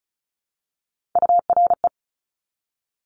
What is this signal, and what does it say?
Morse code, heard as a keyed tone, about 36 wpm, 700 Hz. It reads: URE
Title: Top Three Letter Combinations in Words 35wpm